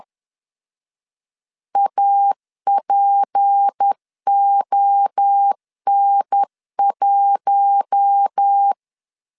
Morse code, heard as a keyed tone, about 10 wpm, 800 Hz. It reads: APON1
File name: LEZIONI / 10 SIGNS / 10C DOT 10wpm 2940